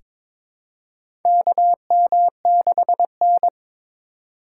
Morse code, heard as a keyed tone, about 22 wpm, 700 Hz. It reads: KM6N
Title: USA Call Signs - 4x Character Spacing - 22wpm